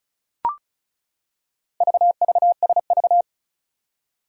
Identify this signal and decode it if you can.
Morse code, heard as a keyed tone, about 36 wpm, 700 Hz. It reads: VVSV
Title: Four Challenging Letter-Number Combinations 35wpm